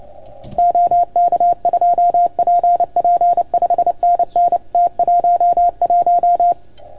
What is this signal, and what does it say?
Morse code, heard as a keyed tone, about 29 wpm, 690 Hz, with radio noise. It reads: OK2PP 5NN T11
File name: Archiv / CUC310 / OK1FGD CUC 310 pileup